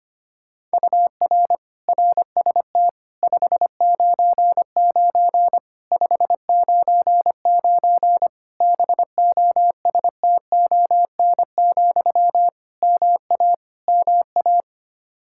Morse code, heard as a keyed tone, about 25 wpm, 700 Hz. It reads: UR RST 599 599 BOSTON, MA MA
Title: Practice QSO - Exchange 1 - Beginning - 25wpm